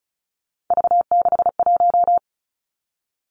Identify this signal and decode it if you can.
Morse code, heard as a keyed tone, about 36 wpm, 700 Hz. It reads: V61
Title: Three Challenging Letter-Number Combinations 35wpm